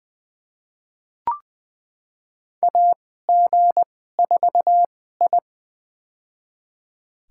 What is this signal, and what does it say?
Morse code, heard as a keyed tone, about 20 wpm, spaced as for 15 wpm, 700 Hz. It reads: AG4I
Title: USA Call Signs - 4 Characters - 2x Character Spacing - 20wpm